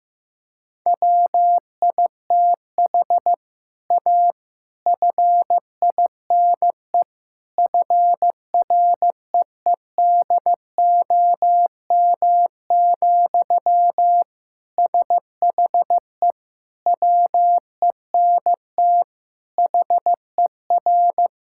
Morse code, heard as a keyed tone, about 15 wpm, 700 Hz. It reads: WITH A FINE FREEDOM, SHE WENT HER
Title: Sentences from Top 1000 Words - Review Part 5 - 15wpm